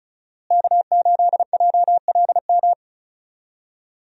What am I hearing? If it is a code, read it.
Morse code, heard as a keyed tone, about 36 wpm, 700 Hz. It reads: K8JLM
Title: USA Call Signs - 5 Characters - Repeated 2x - 35wpm